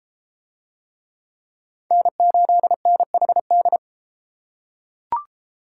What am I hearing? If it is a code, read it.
Morse code, heard as a keyed tone, about 33 wpm, 700 Hz. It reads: N8NHD